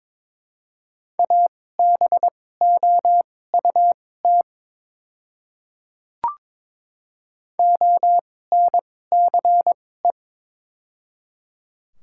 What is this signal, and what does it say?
Morse code, heard as a keyed tone, about 22 wpm, spaced as for 16 wpm, 700 Hz. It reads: ABOUT ONCE
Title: 2x Character Spacing - Top 500 Words - 22wpm